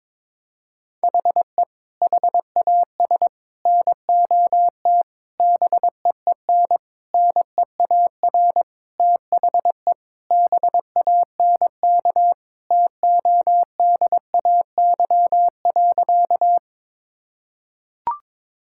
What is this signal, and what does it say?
Morse code, heard as a keyed tone, about 22 wpm, 700 Hz. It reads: HE HAS NOT BEEN NEAR THE BANK TODAY.
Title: Sentences from Top 1000 Words - Review Part 6 - 22wpm